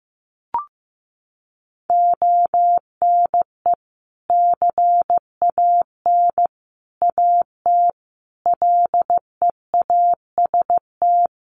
Morse code, heard as a keyed tone, about 15 wpm, 700 Hz. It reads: ONE CAN AT LEAST
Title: Sentences from Top 1200 Words 15wpm